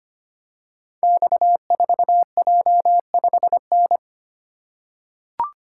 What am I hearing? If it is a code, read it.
Morse code, heard as a keyed tone, about 25 wpm, 700 Hz. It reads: X4J5N
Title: Code Groups - 5 Letter:Numbers - 25wpm